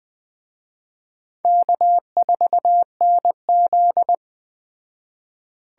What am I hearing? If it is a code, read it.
Morse code, heard as a keyed tone, about 21 wpm, 700 Hz. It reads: K4NZ